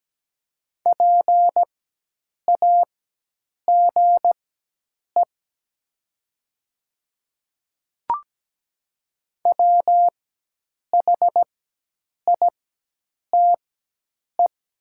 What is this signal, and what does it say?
Morse code, heard as a keyed tone, about 17 wpm, spaced as for 8 wpm, 700 Hz. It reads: PAGE WHITE
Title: 4x Character Spacing - Top 300 Words - 17wpm